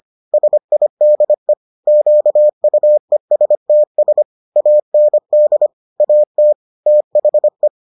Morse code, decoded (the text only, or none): SIDE QUESTS AND AT THE